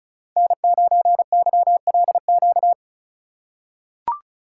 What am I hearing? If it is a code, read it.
Morse code, heard as a keyed tone, about 36 wpm, 700 Hz. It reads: N9YLQ
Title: USA Call Signs - 5 Characters - 3x Character Spacing - 35wpm